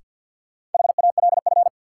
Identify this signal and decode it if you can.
Morse code, heard as a keyed tone, about 49 wpm, 700 Hz. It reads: HALF
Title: Sentences from Top 400 Words - New Word Drill - 50wpm